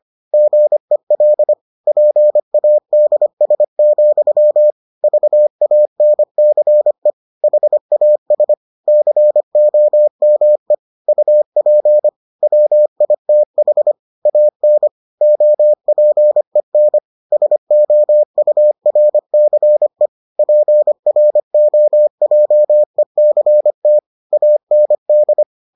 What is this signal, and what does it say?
Morse code, heard as a keyed tone, about 25 wpm, 600 Hz. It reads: GEL PADS, VANCE HAS COME UP WITH AN OPEN SOURCE PROJECT AND